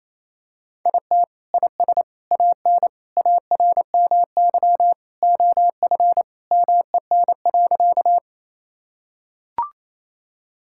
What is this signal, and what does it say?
Morse code, heard as a keyed tone, about 28 wpm, 700 Hz. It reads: IT IS AN ARMY OF MEN.
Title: Sentences from Top 1000 Words - Review Part 5 - 28wpm